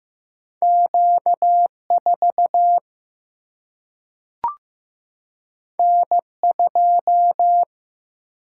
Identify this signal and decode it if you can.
Morse code, heard as a keyed tone, about 15 wpm, 700 Hz. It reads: Q4 N2
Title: Single-Letter Single-Number 15wpm